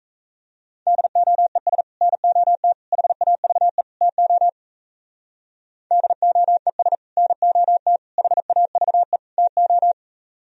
Morse code, heard as a keyed tone, about 37 wpm, 700 Hz. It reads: DOES NOT HAVE TO DOES NOT HAVE TO